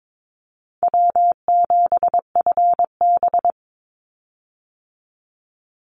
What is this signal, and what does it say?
Morse code, heard as a keyed tone, about 22 wpm, 700 Hz. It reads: W7FB
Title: Call Signs - 22wpm